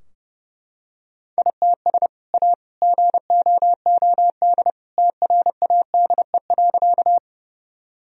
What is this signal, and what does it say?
Morse code, heard as a keyed tone, about 32 wpm, 700 Hz. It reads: ITS A GOOD TRADE.
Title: Sentences from Top 1000 Words - Review Part 4 - 30wpm